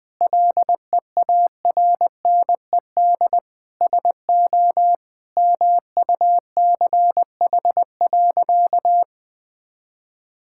Morse code, heard as a keyed tone, about 20 wpm, 700 Hz. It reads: LEARNED SO MUCH.
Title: Sentences with 5-Word Phrases - 20wpm